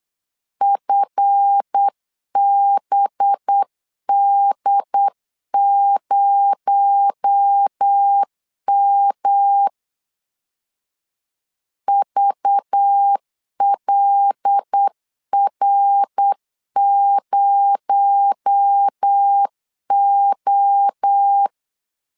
Morse code, heard as a keyed tone, about 8 wpm, 800 Hz. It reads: FBD0M VLR0O
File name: LEZIONI / 7 GF9Z / 7 GF 8wpm 3108